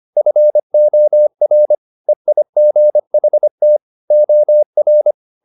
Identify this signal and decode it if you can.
Morse code, heard as a keyed tone, about 25 wpm, 600 Hz. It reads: FOR EIGHT OR